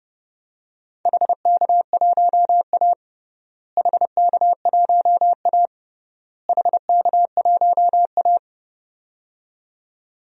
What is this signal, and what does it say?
Morse code, heard as a keyed tone, about 32 wpm, 700 Hz. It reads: HK1A HK1A HK1A
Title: DXCC-100 Call Signs - 4 Characters - Repeated 3x - 30wpm